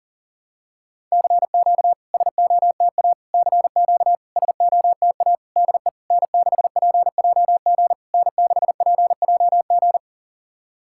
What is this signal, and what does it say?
Morse code, heard as a keyed tone, about 44 wpm, 700 Hz. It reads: CQ SOTA CQ SOTA DE N6PJG N6PJG
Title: SOTA QSO Practice 40wpm